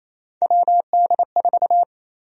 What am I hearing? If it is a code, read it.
Morse code, heard as a keyed tone, about 28 wpm, 700 Hz. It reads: WD4
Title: USA Call Sign Prefixes - 28wpm